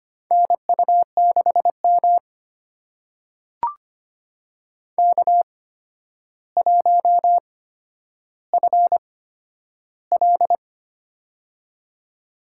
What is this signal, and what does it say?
Morse code, heard as a keyed tone, about 25 wpm, spaced as for 11 wpm, 700 Hz. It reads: NU6M K1FL
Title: USA Call Signs - 4 Characters - 8x Character Spacing - 25wpm